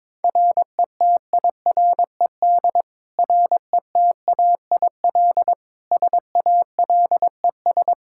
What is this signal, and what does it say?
Morse code, heard as a keyed tone, about 22 wpm, 700 Hz. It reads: RETIRED RETAIL SALES
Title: Practice QSO - Exchange 3 - More Info - 22wpm